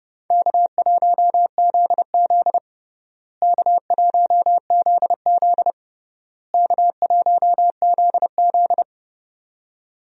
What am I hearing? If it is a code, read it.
Morse code, heard as a keyed tone, about 32 wpm, 700 Hz. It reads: K1ZZ K1ZZ K1ZZ
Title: USA Call Signs - 4 Characters - Repeated 3x - 30wpm